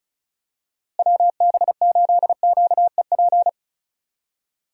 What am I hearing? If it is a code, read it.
Morse code, heard as a keyed tone, about 36 wpm, 700 Hz. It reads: WB8QEP
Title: USA Call Signs - 6 Characters - Repeated 2x - 35wpm